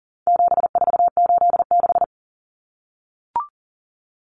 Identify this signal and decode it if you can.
Morse code, heard as a keyed tone, about 40 wpm, 700 Hz. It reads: 7486
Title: Common Cut Numbers - 4-Digit 40wpm